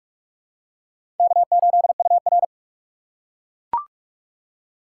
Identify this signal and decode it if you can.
Morse code, heard as a keyed tone, about 42 wpm, 700 Hz. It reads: K8UR
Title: USA Call Signs - 4 Characters - 2x Character Spacing - 45wpm